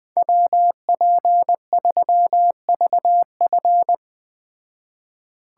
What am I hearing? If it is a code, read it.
Morse code, heard as a keyed tone, about 20 wpm, 700 Hz. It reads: WP3VF